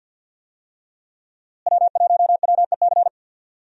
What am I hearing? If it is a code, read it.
Morse code, heard as a keyed tone, about 50 wpm, 700 Hz. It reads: W1WEC
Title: USA Call Signs - 5 Characters - Repeated 3x - 50wpm